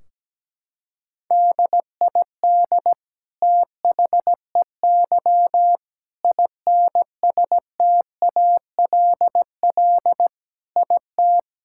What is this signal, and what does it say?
Morse code, heard as a keyed tone, about 17 wpm, 700 Hz. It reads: DID THEY INSTALL IT
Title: Sentences from Top 1600 Words 17wpm